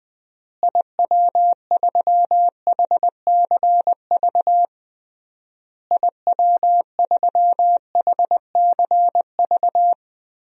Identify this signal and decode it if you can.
Morse code, heard as a keyed tone, about 20 wpm, 700 Hz. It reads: IW3HCV IW3HCV
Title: DXCC-100 Call Signs - 6 Characters - Repeated 3x - 20wpm